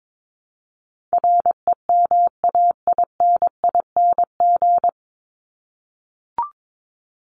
Morse code, heard as a keyed tone, about 22 wpm, 700 Hz. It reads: REMAINING
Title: ING Suffix - Top 100 Words with Base Words 22wpm